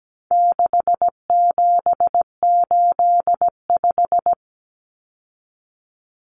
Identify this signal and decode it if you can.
Morse code, heard as a keyed tone, about 17 wpm, 700 Hz. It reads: 6785
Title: Common Cut Numbers - 4-Digit 17wpm